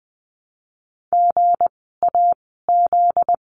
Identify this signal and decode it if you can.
Morse code, heard as a keyed tone, about 20 wpm, 700 Hz. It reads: G A Z
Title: 244 Lesson 39f - Letter z - Words 20wpm